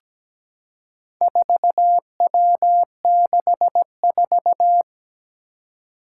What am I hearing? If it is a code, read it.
Morse code, heard as a keyed tone, about 17 wpm, 700 Hz. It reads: 4W64
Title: Four Challenging Letter-Number Combinations 17wpm